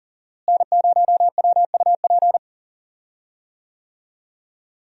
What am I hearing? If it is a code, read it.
Morse code, heard as a keyed tone, about 40 wpm, 700 Hz. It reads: N0WUP
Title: USA Call Signs - 5 Characters - Repeated 2x - 40wpm